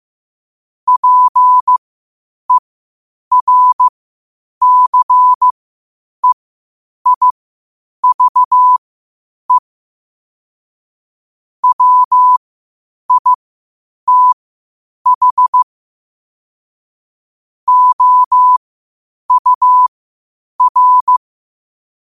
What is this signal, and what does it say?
Morse code, heard as a keyed tone, about 15 wpm, spaced as for 8 wpm, 1000 Hz. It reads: PERCEIVE WITH OUR
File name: qotdfiles / 8wpm